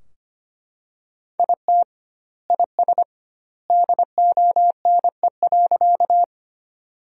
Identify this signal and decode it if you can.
Morse code, heard as a keyed tone, about 25 wpm, 700 Hz. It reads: IT IS DONE.
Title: 2x Word-Spacing - Sentences from Top 500 Words - 25wpm